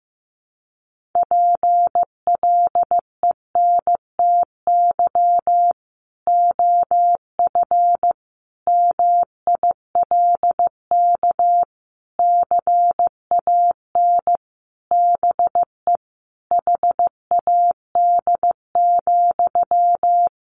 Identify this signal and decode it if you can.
Morse code, heard as a keyed tone, about 14 wpm, 700 Hz. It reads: PLENTY OF MILK CAN BE HAD,
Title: Sentences from Top 1500 Words 15wpm